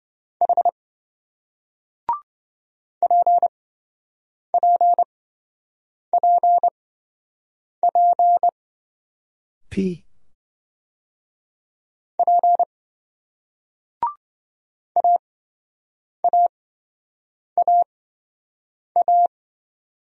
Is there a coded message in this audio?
Yes: Morse code from 0.4 to 19.3 s, heard as a keyed tone, about 26 wpm, spaced as for 7 wpm, 700 Hz. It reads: H PPPP P AAAA